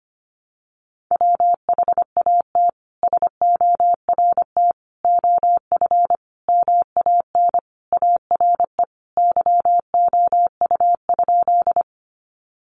Morse code, heard as a keyed tone, about 25 wpm, 700 Hz. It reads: WHAT SORT OF MAN ARE YOU?
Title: Sentences from Top 1300 Words 25wpm